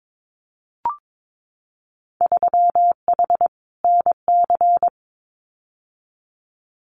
Morse code, heard as a keyed tone, about 22 wpm, 700 Hz. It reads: 3H NC
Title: Winter Field Day Exchanges 22wpm